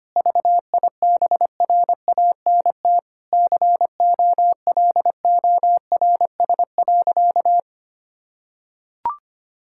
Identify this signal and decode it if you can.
Morse code, heard as a keyed tone, about 25 wpm, 700 Hz. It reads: VIBRANT COLORS.
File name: practice / 25wpm / Difficult Sentences with Hints - 25wpm